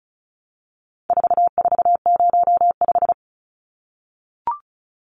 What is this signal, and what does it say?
Morse code, heard as a keyed tone, about 34 wpm, 700 Hz. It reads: 4405
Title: Numbers - 4-Digit 35wpm